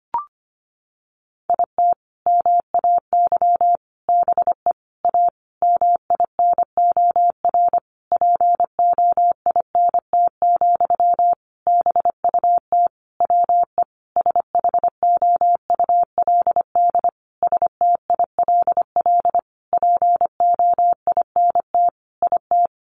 Morse code, heard as a keyed tone, about 25 wpm, 700 Hz. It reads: IT MAY BE A MINOR POINT, BUT WE SHOULD STILL POINT IT
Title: Sentences from Top 1500 Words 25wpm